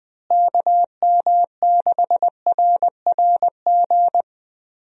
Morse code, heard as a keyed tone, about 21 wpm, 700 Hz. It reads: KM6RRG
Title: USA Call Signs - 6 Characters - Repeated 2x - 20wpm